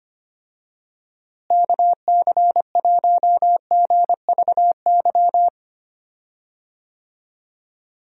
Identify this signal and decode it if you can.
Morse code, heard as a keyed tone, about 25 wpm, 700 Hz. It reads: KC1GVY
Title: USA Call Signs - 6 Characters - Repeated 2x - 25wpm